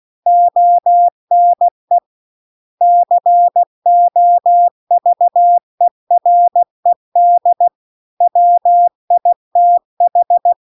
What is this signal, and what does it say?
Morse code, heard as a keyed tone, about 16 wpm, 700 Hz. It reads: ONE COVERED WITH